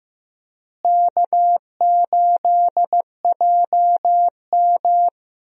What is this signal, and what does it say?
Morse code, heard as a keyed tone, about 15 wpm, 700 Hz. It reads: K8JM
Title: USA Call Signs - 4 Characters - Repeated 3x - 15wpm